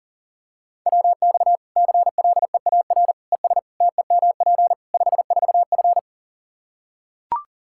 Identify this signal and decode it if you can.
Morse code, heard as a keyed tone, about 40 wpm, 700 Hz. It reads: WX CLEAR ES TEMP 54F
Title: Practice QSO - Exchange 2 - Info - 40wpm